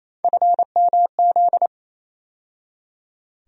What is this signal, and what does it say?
Morse code, heard as a keyed tone, about 27 wpm, 700 Hz. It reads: FMZ
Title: USA Call Sign Postfixes - 3 Letters 28wpm